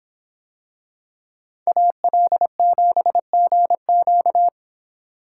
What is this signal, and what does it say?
Morse code, heard as a keyed tone, about 24 wpm, 700 Hz. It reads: AL7GQ